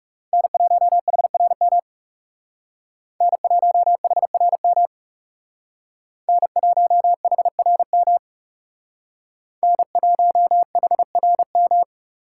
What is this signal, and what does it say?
Morse code, heard as a keyed tone, about 34 wpm, 700 Hz. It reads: N1HRM N1HRM N1HRM N1HRM
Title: Speed Racing - USA 1x3 Call Signs 30wpm